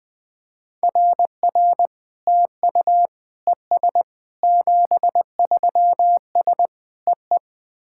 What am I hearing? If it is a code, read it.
Morse code, heard as a keyed tone, about 20 wpm, 690 Hz. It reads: RR TU ES 73S EE
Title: SOTA QSO Practice 20wpm